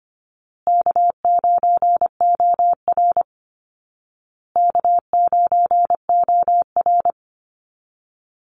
Morse code, heard as a keyed tone, about 25 wpm, 700 Hz. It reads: K9OR K9OR
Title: Call Signs Repeated 2x - 25wpm